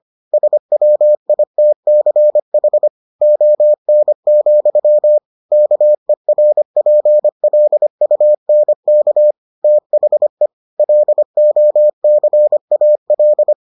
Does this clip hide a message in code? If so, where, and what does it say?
Yes: Morse code from 0.3 to 13.5 s, heard as a keyed tone, about 25 wpm, 600 Hz. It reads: SWITCH ON, KERPLUNK THE LOCAL